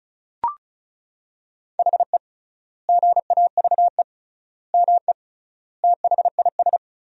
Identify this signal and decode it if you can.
Morse code, heard as a keyed tone, about 34 wpm, 700 Hz. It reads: HE GAVE ME THIS
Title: 3x Word-Spacing - Sentences from Top 500 Words - 35wpm